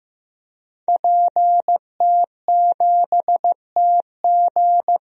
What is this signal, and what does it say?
Morse code, heard as a keyed tone, about 15 wpm, 700 Hz. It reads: PT7TG
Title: DXCC-100 Call Signs - 5 Characters - Repeated 2x - 15wpm